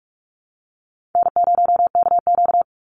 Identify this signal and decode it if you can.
Morse code, heard as a keyed tone, about 46 wpm, 700 Hz. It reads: N0KQ
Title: Easy Call Signs Repeated 3x - 45wpm